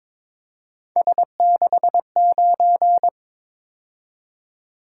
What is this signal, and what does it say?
Morse code, heard as a keyed tone, about 22 wpm, 700 Hz. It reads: S69
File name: practice / 22wpm / Three Letter-Number Combinations - 22wpm